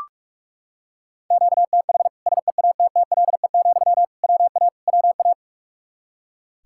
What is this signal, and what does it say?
Morse code, heard as a keyed tone, about 42 wpm, 700 Hz. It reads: QTH SEATTLE, WA WA
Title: Practice QSO - Exchange 1 - Beginning - 45wpm